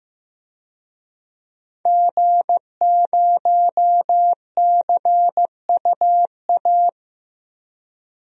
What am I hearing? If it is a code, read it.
Morse code, heard as a keyed tone, about 15 wpm, 700 Hz. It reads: G0CUA
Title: DXCC-100 Call Signs - 5 Characters - Repeated 3x - 15wpm